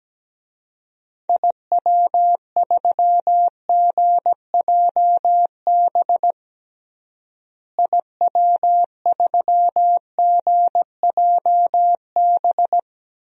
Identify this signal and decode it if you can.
Morse code, heard as a keyed tone, about 17 wpm, 700 Hz. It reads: IW3GJB IW3GJB
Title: DXCC-100 Call Signs - 6 Characters - Repeated 2x - 17wpm